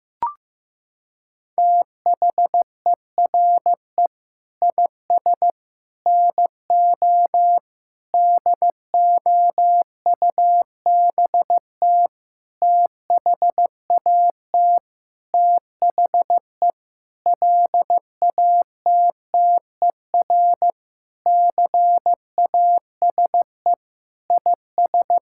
Morse code, heard as a keyed tone, about 16 wpm, 700 Hz. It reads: THERE IS NO DOUBT THAT THE LATTER CASE IS
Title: Sentences from Top 1600 Words 15wpm